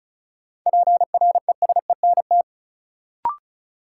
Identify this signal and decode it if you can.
Morse code, heard as a keyed tone, about 36 wpm, 700 Hz. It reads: PRESENT
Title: Sentences from Top 800 Words - New Word Drill - 35wpm